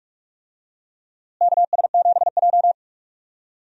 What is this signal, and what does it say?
Morse code, heard as a keyed tone, about 44 wpm, 700 Hz. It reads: KS7J
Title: USA Call Signs - 4 Characters - Repeated 2x - 45wpm